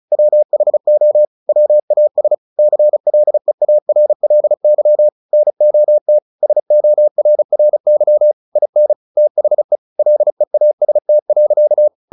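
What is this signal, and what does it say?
Morse code, heard as a keyed tone, about 34 wpm, 600 Hz. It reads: WHO WAS CLEARLY NOT SORRY IN THE LEAST.
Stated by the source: Morse practice at 35 wpm from CW med Gnister, sending Ken Follett's novel Never in English.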